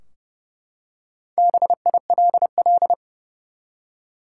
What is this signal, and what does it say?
Morse code, heard as a keyed tone, about 30 wpm, 700 Hz. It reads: BILL
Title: Four Letter Words - 30wpm